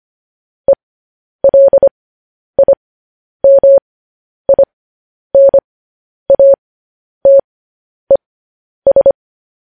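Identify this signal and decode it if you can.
Morse code, heard as a keyed tone, about 25 wpm, spaced as for 10 wpm, 550 Hz. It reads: ELIMINATES
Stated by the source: Morse code news headlines broadcast.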